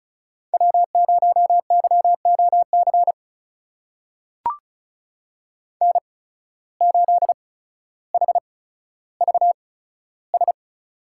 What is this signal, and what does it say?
Morse code, heard as a keyed tone, about 34 wpm, spaced as for 15 wpm, 700 Hz. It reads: W0YOC N8HVS
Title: USA Call Signs - 5 Characters - 8x Character Spacing - 35wpm